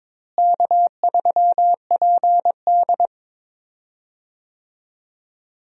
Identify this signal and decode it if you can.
Morse code, heard as a keyed tone, about 22 wpm, 700 Hz. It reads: K3PD
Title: USA Call Signs - 4 Characters - Repeated 3x - 22wpm